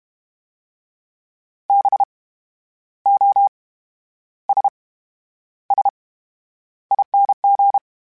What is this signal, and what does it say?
Morse code, heard as a keyed tone, about 32 wpm, spaced as for 11 wpm, 800 Hz. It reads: BOSSING
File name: Special INGs / Special INGs-mp3s / VerySlowToFaster Ing @30